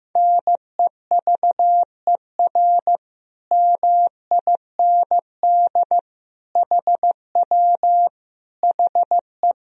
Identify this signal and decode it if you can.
Morse code, heard as a keyed tone, about 15 wpm, 700 Hz. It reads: NEVER MIND HW HE
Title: Sentences from Top 500 Words and Abbreviations 15wpm